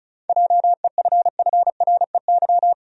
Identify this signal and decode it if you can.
Morse code, heard as a keyed tone, about 34 wpm, 700 Hz. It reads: JEFFREY
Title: USA Common Names - 35wpm